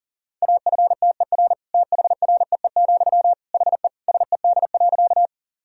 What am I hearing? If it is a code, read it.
Morse code, heard as a keyed tone, about 40 wpm, 700 Hz. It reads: AFTER THREE, HE SED.
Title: Sentences from Top 200 Words and Abbreviations 40wpm